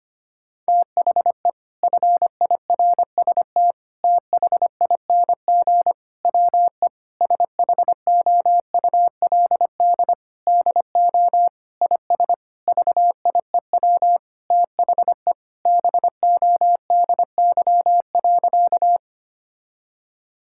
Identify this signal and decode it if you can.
Morse code, heard as a keyed tone, about 25 wpm, 700 Hz. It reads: THE FIRST THING WE SHOULD DO IS VIEW THE BODY.
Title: Sentences from Top 1000 Words - Review Part 6 - 25wpm